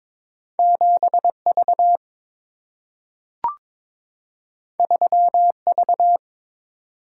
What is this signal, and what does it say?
Morse code, heard as a keyed tone, about 22 wpm, 700 Hz. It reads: 7 V 3 V
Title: Single-Letter Single-Number 22wpm